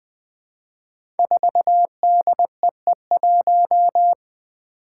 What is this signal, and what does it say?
Morse code, heard as a keyed tone, about 20 wpm, 700 Hz. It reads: 4DEE1
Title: Five Challenging Letter-Number Combinations 20wpm